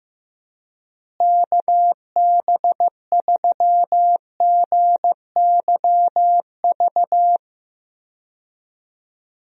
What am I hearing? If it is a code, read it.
Morse code, heard as a keyed tone, about 15 wpm, 700 Hz. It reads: KB3GYV
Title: USA Call Signs - 6 Characters - Repeated 2x - 15wpm